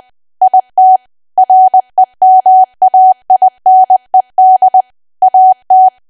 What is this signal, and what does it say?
Morse code, heard as a keyed tone, about 20 wpm, 750 Hz. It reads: IT REMAINED AT
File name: Archive / 20 WPM / 20WPM